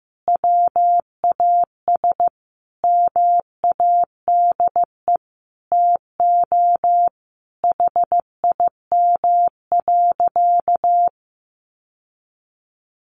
Morse code, heard as a keyed tone, about 15 wpm, 700 Hz. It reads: WAS MADE TO HIM.
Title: Sentences from Top 1300 Words 15wpm